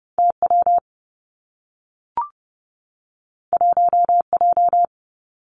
Morse code, heard as a keyed tone, about 30 wpm, 700 Hz. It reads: T W 1 J
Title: Two Challenging Letter-Number Combinations 30wpm